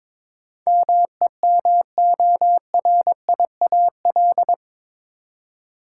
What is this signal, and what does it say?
Morse code, heard as a keyed tone, about 22 wpm, 700 Hz. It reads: MEMORIAL
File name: practice / 22wpm / Eight Letter Words - 22wpm